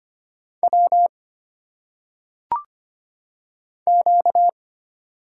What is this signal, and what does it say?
Morse code, heard as a keyed tone, about 25 wpm, 700 Hz. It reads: W Q
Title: Single Letters - Phonetic Pronunciation 25wpm